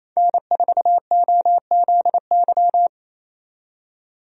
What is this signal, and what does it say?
Morse code, heard as a keyed tone, about 28 wpm, 700 Hz. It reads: N4OZY
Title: USA Call Signs - 5 Characters - 28wpm